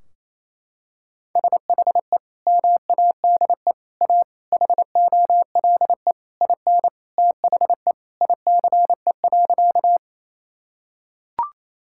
Morse code, heard as a keyed tone, about 28 wpm, 700 Hz. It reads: SHE MADE A HOLE IN THE ICE.
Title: Sentences from Top 1600 Words 28wpm